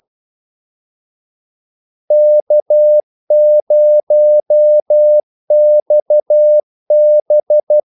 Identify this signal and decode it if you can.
Morse code, heard as a keyed tone, about 12 wpm, 600 Hz. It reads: K0XB
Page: 2 ekstra ordmellemrum mellem kaldesignalerne